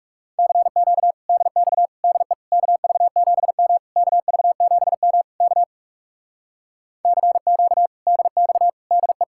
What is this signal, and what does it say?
Morse code, heard as a keyed tone, about 40 wpm, 700 Hz. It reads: CQ DX DE KV7M KV7M K CQ DX DE